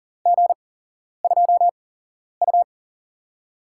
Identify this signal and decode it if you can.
Morse code, heard as a keyed tone, about 40 wpm, spaced as for 11 wpm, 700 Hz. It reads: G2U
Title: USA Call Signs - 8x Character Spacing - 40wpm